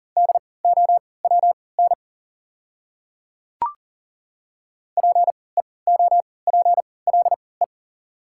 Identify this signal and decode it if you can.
Morse code, heard as a keyed tone, about 40 wpm, spaced as for 23 wpm, 700 Hz. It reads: DOWN PEOPLE
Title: 3x Character Spacing - Top 200 Words - 40wpm